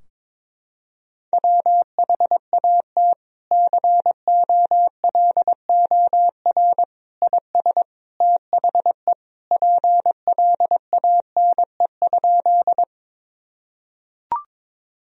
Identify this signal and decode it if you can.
Morse code, heard as a keyed tone, about 22 wpm, 700 Hz. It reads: WHAT COLOR IS THE PLANE?
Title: Sentences from Top 1000 Words - Review Part 3 - 22wpm